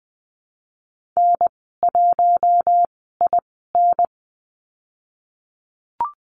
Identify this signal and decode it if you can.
Morse code, heard as a keyed tone, about 20 wpm, spaced as for 15 wpm, 700 Hz. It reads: N1IN
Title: 048 Lesson 9g - Number 4 - Call Signs 20wpm